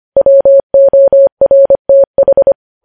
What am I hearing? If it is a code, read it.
Morse code, heard as a keyed tone, about 25 wpm, 550 Hz. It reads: WORTH
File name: podcast / News headlines-1x-25x25